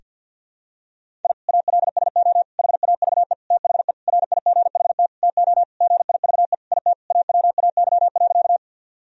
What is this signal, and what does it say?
Morse code, heard as a keyed tone, about 51 wpm, 700 Hz. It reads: I ALSO HAVE THE RIGHT TO GIVE IT AWAY.
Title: Sentences from Top 1000 Words - Review Part 1 - 50wpm